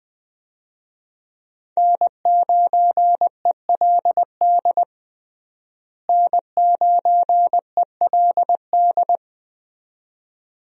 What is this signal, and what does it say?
Morse code, heard as a keyed tone, about 21 wpm, 700 Hz. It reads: N9ELD N9ELD
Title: USA Call Signs - 5 Characters - Repeated 2x - 20wpm